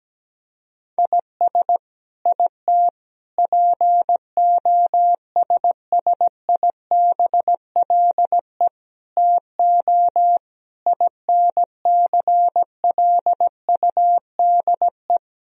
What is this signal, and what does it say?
Morse code, heard as a keyed tone, about 17 wpm, 700 Hz. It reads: IS IT POSSIBLE TO INCLUDE
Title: Sentences from Top 1000 Words - Review Part 3 - 17wpm